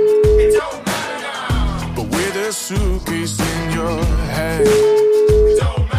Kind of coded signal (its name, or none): telephone call progress tone